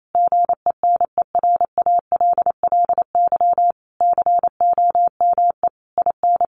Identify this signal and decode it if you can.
Morse code, heard as a keyed tone, about 28 wpm, 700 Hz. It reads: GENERALLY COME IN
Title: Sentences from Top 1500 Words 28wpm